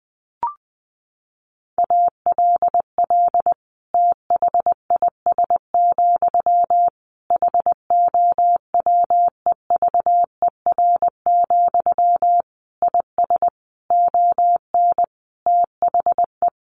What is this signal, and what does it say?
Morse code, heard as a keyed tone, about 20 wpm, 700 Hz. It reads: ALL THIS, HOWEVER, IS ON THE
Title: Sentences from Top 600 Words 20wpm